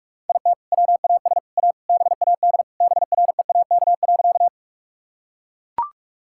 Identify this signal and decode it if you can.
Morse code, heard as a keyed tone, about 41 wpm, 700 Hz. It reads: IT WAS A BAD BREAK.
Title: Sentences from Top 1000 Words - Review Part 6 - 45wpm